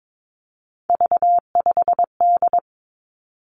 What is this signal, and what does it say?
Morse code, heard as a keyed tone, about 22 wpm, 700 Hz. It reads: V5D
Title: Three Challenging Letter-Number Combinations 22wpm